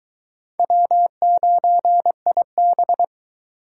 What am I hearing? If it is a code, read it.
Morse code, heard as a keyed tone, about 22 wpm, 700 Hz. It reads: W9IB